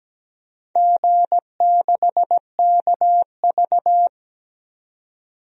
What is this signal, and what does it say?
Morse code, heard as a keyed tone, about 17 wpm, 700 Hz. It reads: G6KV
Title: DXCC-100 Call Signs - 4 Characters - 17wpm